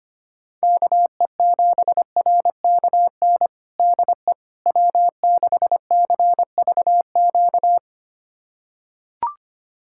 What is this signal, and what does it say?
Morse code, heard as a keyed tone, about 25 wpm, 700 Hz. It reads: KE7RKN DE W6CVQ
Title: USA Callsign DE Callsign 25wpm